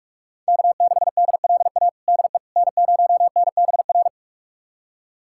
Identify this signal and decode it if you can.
Morse code, heard as a keyed tone, about 42 wpm, 700 Hz. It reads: K6DLA DE N0NBR
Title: USA Callsign DE Callsign - 5 Characters - 45wpm